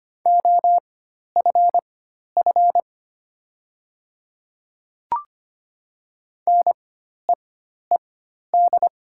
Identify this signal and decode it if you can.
Morse code, heard as a keyed tone, about 25 wpm, spaced as for 12 wpm, 700 Hz. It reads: OFF NEED
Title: 4x Character Spacing - Top 500 Words - 25wpm